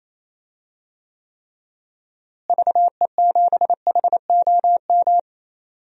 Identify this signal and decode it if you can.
Morse code, heard as a keyed tone, about 28 wpm, 700 Hz. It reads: VE7HOM
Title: DXCC-100 Call Signs - 6 Characters - Repeated 3x - 28wpm